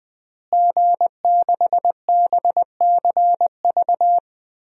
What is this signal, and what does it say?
Morse code, heard as a keyed tone, about 20 wpm, 700 Hz. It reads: G6BCV
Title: DXCC-100 Call Signs - 5 Characters - Repeated 3x - 20wpm